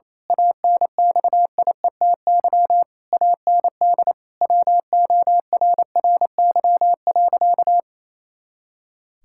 Morse code, heard as a keyed tone, about 28 wpm, 700 Hz. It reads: ANXIETY AND WORRY.
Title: Pangram Challenge v2 - 28wpm